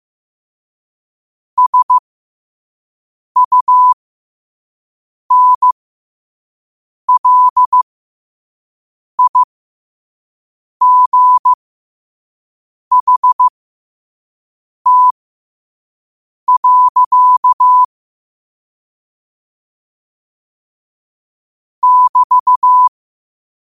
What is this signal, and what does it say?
Morse code, heard as a keyed tone, about 15 wpm, spaced as for 5 wpm, 1000 Hz. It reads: SUNLIGHT. <BT>
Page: New quotes every day in morse code at 5 Words per minute.